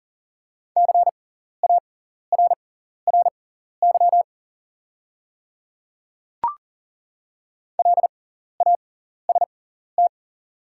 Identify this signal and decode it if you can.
Morse code, heard as a keyed tone, about 37 wpm, spaced as for 14 wpm, 690 Hz. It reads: CARRY LAST
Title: 6x Character Spacing - Top 300 Words - 40wpm